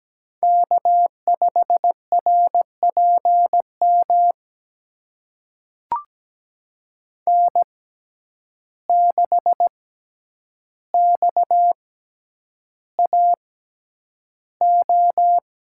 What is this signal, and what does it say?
Morse code, heard as a keyed tone, about 17 wpm, spaced as for 9 wpm, 700 Hz. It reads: K5RPM N6XAO
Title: USA Call Signs - 5 Characters - 6x Character Spacing - 17wpm